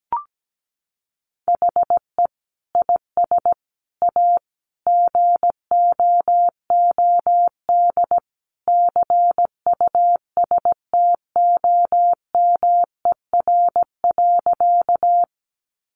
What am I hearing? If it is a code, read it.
Morse code, heard as a keyed tone, about 17 wpm, 700 Hz. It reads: HE IS A GOOD CUSTOMER.
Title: Sentences from Top 1100 Words 17wpm